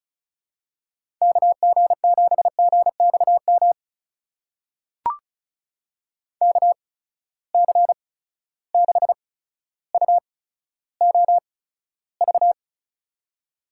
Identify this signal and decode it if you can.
Morse code, heard as a keyed tone, about 36 wpm, spaced as for 15 wpm, 700 Hz. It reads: KG7GXM KC6UOV
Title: USA Call Signs - 6 Characters - 8x Character Spacing - 35wpm